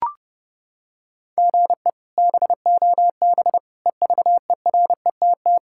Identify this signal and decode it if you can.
Morse code, heard as a keyed tone, about 30 wpm, 700 Hz. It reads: GE BOB EVERETT
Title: K1USN SST QSO Practice - 30wpm